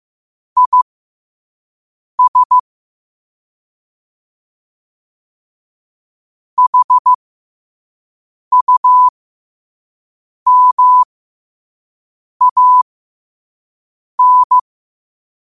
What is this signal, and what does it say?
Morse code, heard as a keyed tone, about 15 wpm, spaced as for 5 wpm, 1000 Hz. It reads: IS HUMAN